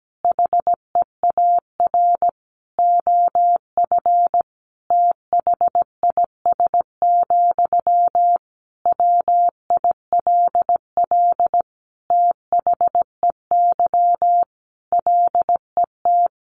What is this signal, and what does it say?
Morse code, heard as a keyed tone, about 17 wpm, 700 Hz. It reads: HEAR OF THIS, WILL THEY LET
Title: Sentences from Top 400 Words 17wpm